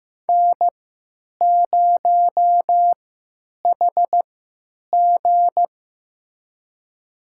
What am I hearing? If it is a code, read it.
Morse code, heard as a keyed tone, about 16 wpm, spaced as for 9 wpm, 700 Hz. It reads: N0HG
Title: USA Call Signs - 4 Characters - 3x Character Spacing - 15wpm